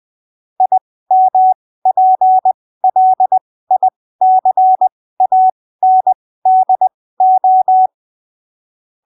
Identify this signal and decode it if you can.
Morse code, heard as a keyed tone, about 20 wpm, spaced as for 15 wpm, 750 Hz. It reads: IMPLICANDO
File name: arquivos / texto / texto 22 20ppm